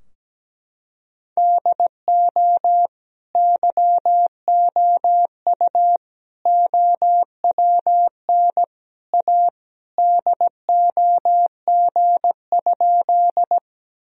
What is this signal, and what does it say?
Morse code, heard as a keyed tone, about 17 wpm, 700 Hz. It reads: DO YOU OWN A DOG?
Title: Sentences from Top 1000 Words - Review Part 2 - 17wpm